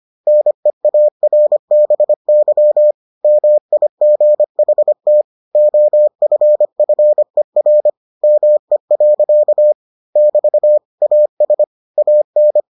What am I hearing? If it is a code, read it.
Morse code, heard as a keyed tone, about 25 wpm, 600 Hz. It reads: NEARBY MIGHT OFFER ME. <BT> AS AN